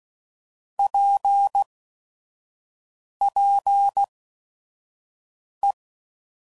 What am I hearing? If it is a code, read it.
Morse code, heard as a keyed tone, about 16 wpm, spaced as for 5 wpm, 800 Hz. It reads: PPE
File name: CW Code Groups @5WPM / Technical Subjects / XC